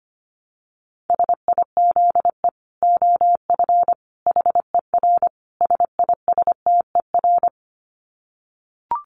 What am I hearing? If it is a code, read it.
Morse code, heard as a keyed tone, about 25 wpm, 700 Hz. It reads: SIZE OF HER SISTER
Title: Dit Challenge 25wpm